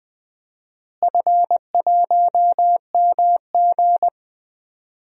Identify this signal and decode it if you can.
Morse code, heard as a keyed tone, about 20 wpm, 700 Hz. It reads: F1MG
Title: DXCC-100 Call Signs - 4 Characters - Repeated 3x - 20wpm